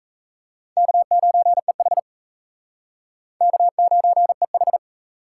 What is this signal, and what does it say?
Morse code, heard as a keyed tone, about 34 wpm, 700 Hz. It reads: K9EH K9EH
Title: Speed Racing - USA 1x2 Call Signs 28wpm